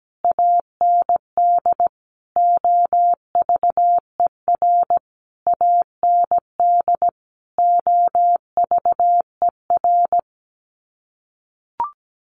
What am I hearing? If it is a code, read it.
Morse code, heard as a keyed tone, about 17 wpm, 700 Hz. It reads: AND OVER AND OVER
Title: Sets of 5 Words Encore 17wpm